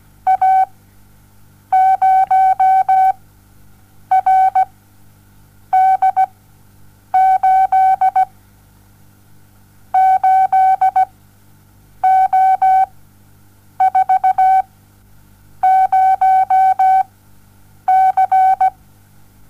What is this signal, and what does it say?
Morse code, heard as a keyed tone, about 17 wpm, spaced as for 7 wpm, 760 Hz. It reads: A0RD8 8O40C
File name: Onlinelehrgang / morsen / DJ4UF Morsen07